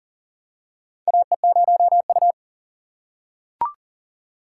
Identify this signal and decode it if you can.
Morse code, heard as a keyed tone, about 40 wpm, 700 Hz. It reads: AE0U
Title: USA Call Signs - 4 Characters - 2x Character Spacing - 40wpm